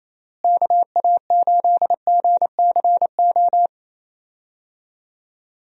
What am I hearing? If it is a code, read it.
Morse code, heard as a keyed tone, about 28 wpm, 700 Hz. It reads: KA8GCO
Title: USA Call Signs - 6 Characters - Repeated 2x - 28wpm